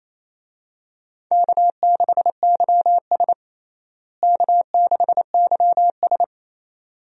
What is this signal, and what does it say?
Morse code, heard as a keyed tone, about 27 wpm, 700 Hz. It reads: K6YS K6YS
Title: USA Call Signs - 4 Characters - Repeated 2x - 28wpm